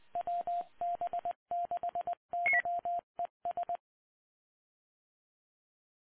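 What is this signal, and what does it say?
Morse code, heard as a keyed tone, about 20 wpm, 700 Hz. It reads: WB6YES
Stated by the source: Morse ident of the PAPA System analog repeaters, WD6FZA.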